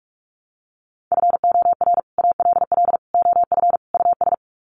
Morse code, heard as a keyed tone, about 44 wpm, 700 Hz. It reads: FOR ALL OF US
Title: Sets of 4 Words 45wpm